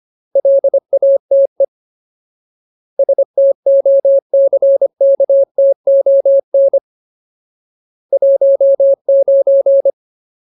Morse code, heard as a keyed tone, about 25 wpm, 550 Hz. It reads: LATE STOCKTON 19
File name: Ultimate Word List / 25WPM Ultimate